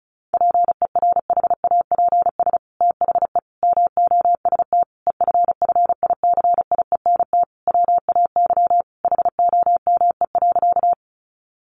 Morse code, heard as a keyed tone, about 34 wpm, 700 Hz. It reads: PERHAPS THE MOST EFFICIENT WAY HOME.
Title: Sentences from Top 1300 Words 35wpm